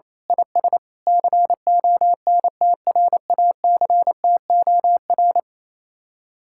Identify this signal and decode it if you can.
Morse code, heard as a keyed tone, about 26 wpm, 700 Hz. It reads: IS CONTRACTOR
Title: Speed-Racing Common QSO Phrases Encore - 28wpm